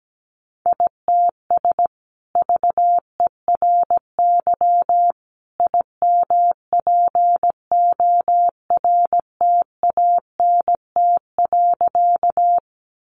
Morse code, heard as a keyed tone, about 17 wpm, 700 Hz. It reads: ITS VERY IMPORTANT.
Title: Sentences from Top 600 Words 17wpm